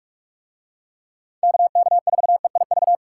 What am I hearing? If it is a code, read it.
Morse code, heard as a keyed tone, about 44 wpm, 700 Hz. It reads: KK4EIV